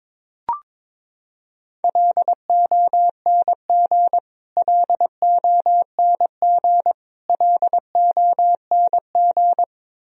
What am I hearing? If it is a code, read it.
Morse code, heard as a keyed tone, about 22 wpm, 700 Hz. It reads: LONG LONG LONG
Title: Top 100 Words Repeated 22wpm